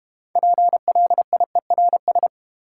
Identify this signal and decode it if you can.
Morse code, heard as a keyed tone, about 30 wpm, 690 Hz. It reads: PLIERS
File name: Easy Words - Speed-Racing with Hints - 25wpm